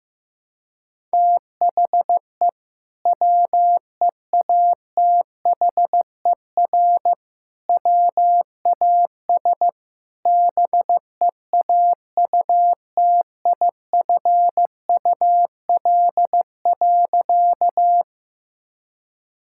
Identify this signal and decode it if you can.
Morse code, heard as a keyed tone, about 14 wpm, 700 Hz. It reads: THE WEATHER WAS BEAUTIFUL.
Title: Sentences from Top 1000 Words - Review Part 5 - 15wpm